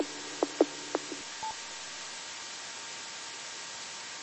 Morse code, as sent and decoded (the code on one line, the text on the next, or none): none